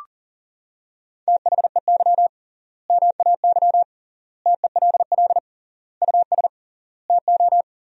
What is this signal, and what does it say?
Morse code, heard as a keyed tone, about 37 wpm, 700 Hz. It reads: THEY MAY TELL US TO
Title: 3x Word-Spacing - Sentences from Top 300 Words - 40wpm